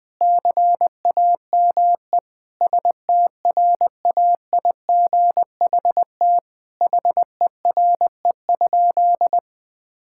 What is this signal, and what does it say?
Morse code, heard as a keyed tone, about 20 wpm, 700 Hz. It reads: CAME STRAIGHT HERE?
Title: Sentences from Top 1000 Words - Review Part 6 - 20wpm